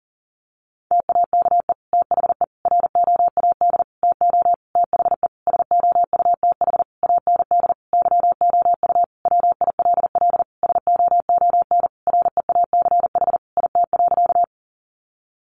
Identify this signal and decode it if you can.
Morse code, heard as a keyed tone, about 40 wpm, 700 Hz. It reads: TAKE THE ROAD TO THE SOUTH AND YOU WILL SOON REACH IT.
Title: Sentences from Top 400 Words 40wpm